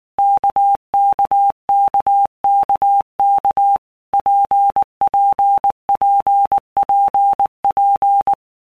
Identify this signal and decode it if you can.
Morse code, heard as a keyed tone, about 19 wpm, 800 Hz. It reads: KKKKK PPPPP
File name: All Groups 1-minute Blocks / Groups@18 / RKPX